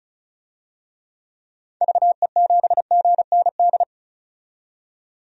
Morse code, heard as a keyed tone, about 36 wpm, 700 Hz. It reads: VE7GND
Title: DXCC-100 Call Signs - 6 Characters - 35wpm